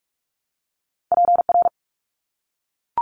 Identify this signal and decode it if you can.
Morse code, heard as a keyed tone, about 44 wpm, 700 Hz. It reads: PR
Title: ARRL Sections 45wpm